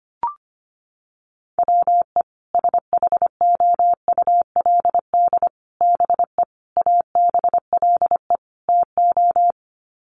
Morse code, heard as a keyed tone, about 25 wpm, 700 Hz. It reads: WE SHOULD BE ABLE TO
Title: Sets of 5 Words Encore2 25wpm